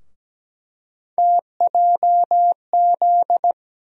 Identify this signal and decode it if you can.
Morse code, heard as a keyed tone, about 17 wpm, 700 Hz. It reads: TJZ
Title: USA Call Sign Postfixes - 3 Letters 17wpm